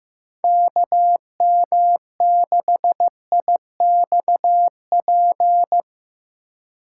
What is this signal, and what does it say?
Morse code, heard as a keyed tone, about 15 wpm, 700 Hz. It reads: KM6IXP
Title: USA Call Signs - 4x Character Spacing - 15wpm